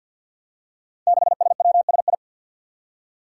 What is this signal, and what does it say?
Morse code, heard as a keyed tone, about 51 wpm, 700 Hz. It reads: 6SWSI
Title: Five Challenging Letter-Number Combinations 50wpm